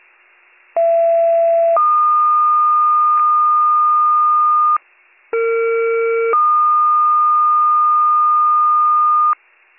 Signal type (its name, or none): two-tone page